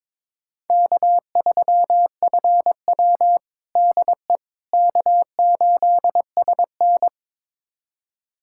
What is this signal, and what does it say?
Morse code, heard as a keyed tone, about 22 wpm, 700 Hz. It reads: K3FW DE K8SN
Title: USA Callsign DE Callsign - 4 Characters - 22wpm